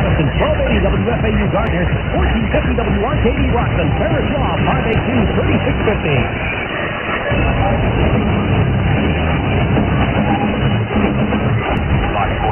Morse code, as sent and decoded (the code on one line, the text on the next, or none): none